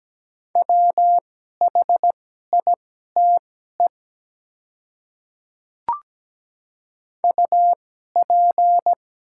Morse code, heard as a keyed tone, about 17 wpm, spaced as for 12 wpm, 700 Hz. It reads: WHITE UP
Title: 2x Character Spacing - Top 300 Words - 17wpm